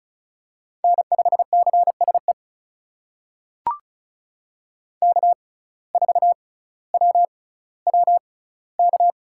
Morse code, heard as a keyed tone, about 36 wpm, spaced as for 18 wpm, 700 Hz. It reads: N5CSE K4WWK
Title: USA Call Signs - 5 Characters - 6x Character Spacing - 35wpm